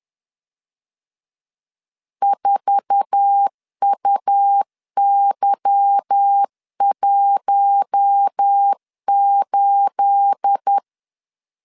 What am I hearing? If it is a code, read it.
Morse code, heard as a keyed tone, about 10 wpm, 800 Hz. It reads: 4UY18
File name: CW / DET / 1 9 1500C S A 10wpm 4511